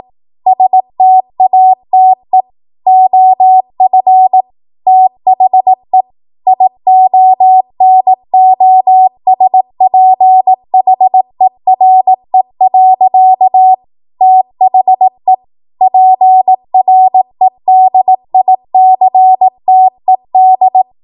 Morse code, heard as a keyed tone, about 18 wpm, 750 Hz. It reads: STATE OF THE IONOSPHERE. THE PREDICTED